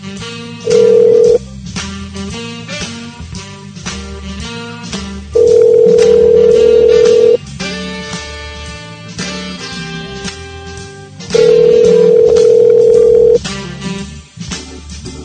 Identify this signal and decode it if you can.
Ringback tone (North America), 440 + 480 Hz.